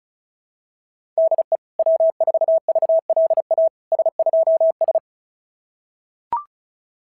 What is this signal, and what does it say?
Morse code, heard as a keyed tone, about 36 wpm, 660 Hz. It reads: DE W4VLA S2S
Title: SOTA QSO Practice 35wpm